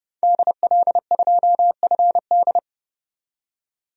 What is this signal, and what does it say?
Morse code, heard as a keyed tone, about 32 wpm, 700 Hz. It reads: DL2FD
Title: DXCC-100 Call Signs - 5 Characters - 30wpm